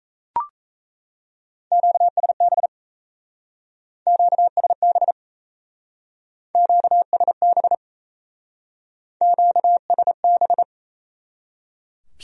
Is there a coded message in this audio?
Yes: Morse code from 1.7 to 10.6 s, heard as a keyed tone, about 37 wpm, 700 Hz. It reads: QSB QSB QSB QSB